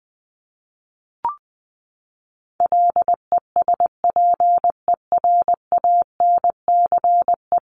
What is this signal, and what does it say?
Morse code, heard as a keyed tone, about 20 wpm, 700 Hz. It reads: LESPERANCE
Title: A Sound Of Thunder 20wpm